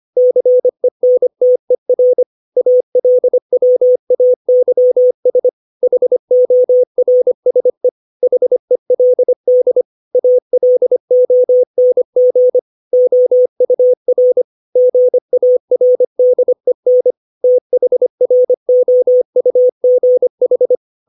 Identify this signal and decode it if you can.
Morse code, heard as a keyed tone, about 25 wpm, 500 Hz. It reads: CENTER ALWAYS HORSE HELD ALONG OUR GARDEN THROUGH
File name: 632 words / 632 Words 25WPM